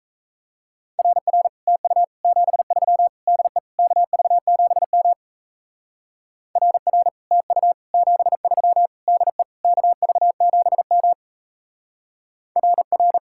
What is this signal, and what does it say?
Morse code, heard as a keyed tone, about 35 wpm, 700 Hz. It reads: RR TU 73 DE KV7M RR TU 73 DE KV7M RR
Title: Speed-Racing Common QSO Phrases - 28wpm